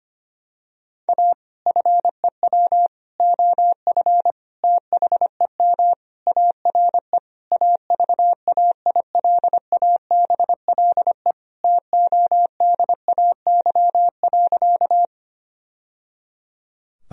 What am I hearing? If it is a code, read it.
Morse code, heard as a keyed tone, about 25 wpm, 700 Hz. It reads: A FEW OF THEM ARE AVAILABLE TODAY.
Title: Sentences from Top 1000 Words - Review Part 3 - 25wpm